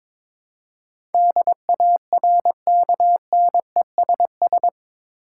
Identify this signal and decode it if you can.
Morse code, heard as a keyed tone, about 22 wpm, 700 Hz. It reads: DARKNESS
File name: practice / 22wpm / Eight Letter Words - 22wpm